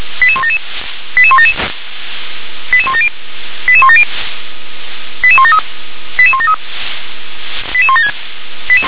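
Five-tone selective call (ZVEI), heard as quick runs of five tones, tones of about 70 ms. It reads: 88177 88177 88170 88170 88173 88173 88176
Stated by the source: Five-tone ZVEI selective call received on an SDR.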